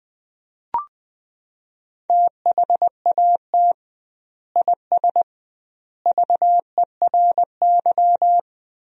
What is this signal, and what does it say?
Morse code, heard as a keyed tone, about 20 wpm, 700 Hz. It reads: THAT IS VERY
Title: 2x Word-Spacing - Sentences from Top 400 Words - 20wpm